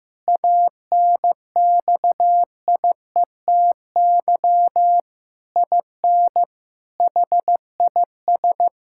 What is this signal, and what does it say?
Morse code, heard as a keyed tone, about 15 wpm, 700 Hz. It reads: ANXIETY IN HIS
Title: Sentences from Top 1600 Words 15wpm